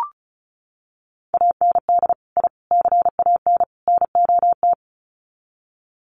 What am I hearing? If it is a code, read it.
Morse code, heard as a keyed tone, about 34 wpm, 700 Hz. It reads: AND I CAN NOT
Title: Sets of 4 Words 35wpm